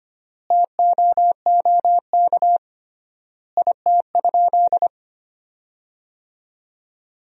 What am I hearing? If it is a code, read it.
Morse code, heard as a keyed tone, about 25 wpm, 700 Hz. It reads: TOOK IT?
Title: 3x Word-Spacing - Sentences from Top 300 Words - 25wpm